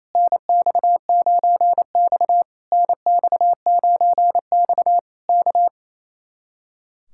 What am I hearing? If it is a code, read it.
Morse code, heard as a keyed tone, about 27 wpm, 700 Hz. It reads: NX9X NX9X K
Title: Speed-Racing Common QSO Phrases - 28wpm